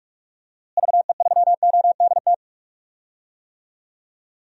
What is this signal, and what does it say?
Morse code, heard as a keyed tone, about 41 wpm, 700 Hz. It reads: VE3ODT
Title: DXCC-100 Call Signs - 6 Characters - Repeated 3x - 45wpm